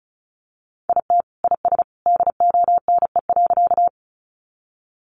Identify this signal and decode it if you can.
Morse code, heard as a keyed tone, about 34 wpm, 700 Hz. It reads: IT IS DONE.
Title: Sentences from Top 500 Words 35wpm